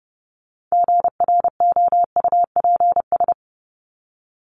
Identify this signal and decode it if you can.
Morse code, heard as a keyed tone, about 30 wpm, 700 Hz. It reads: GROUPS
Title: ES:S Suffix - Top 100 Words with Base Words 30wpm